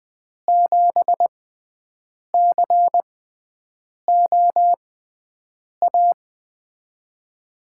Morse code, heard as a keyed tone, about 20 wpm, spaced as for 7 wpm, 700 Hz. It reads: 7COA
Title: USA Call Signs - 6 Characters - 6x Character Spacing - 20wpm